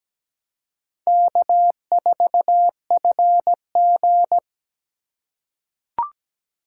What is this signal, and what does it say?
Morse code, heard as a keyed tone, about 17 wpm, 700 Hz. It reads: K4FG
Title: USA Call Signs - 4x Character Spacing - 17wpm